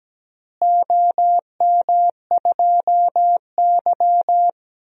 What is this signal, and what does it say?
Morse code, heard as a keyed tone, about 17 wpm, 700 Hz. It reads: OM2Y
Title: DXCC-100 Call Signs - 4 Characters - Repeated 2x - 17wpm